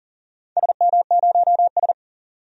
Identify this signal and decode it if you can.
Morse code, heard as a keyed tone, about 40 wpm, 700 Hz. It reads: SM0S
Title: DXCC-100 Call Signs - 4 Characters - 40wpm